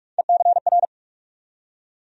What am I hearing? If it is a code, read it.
Morse code, heard as a keyed tone, about 42 wpm, 700 Hz. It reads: ECR
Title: USA Call Sign Postfixes - 3 Letters 45wpm